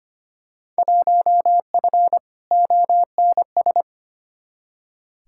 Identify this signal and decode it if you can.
Morse code, heard as a keyed tone, about 25 wpm, 700 Hz. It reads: 1F ONS
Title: Field Day Exchanges Encore - 25wpm